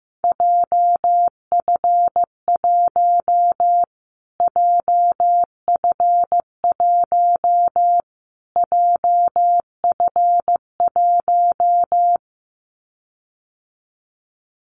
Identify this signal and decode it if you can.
Morse code, heard as a keyed tone, about 15 wpm, 700 Hz. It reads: JF1 JF1 JF1
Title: Callsign Prefixes for Top 25 DXCC - Repeated 3x - 15wpm